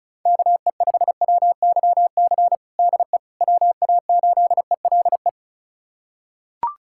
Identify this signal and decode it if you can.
Morse code, heard as a keyed tone, about 34 wpm, 690 Hz. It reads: KE5WYC DE WA8ELE
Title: USA Callsign DE Callsign - 6 Characters - 35wpm